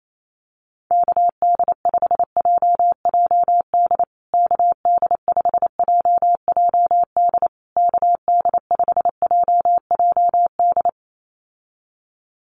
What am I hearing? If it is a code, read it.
Morse code, heard as a keyed tone, about 28 wpm, 700 Hz. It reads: KD5JJD KD5JJD KD5JJD
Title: USA Call Signs Repeated 3x 28wpm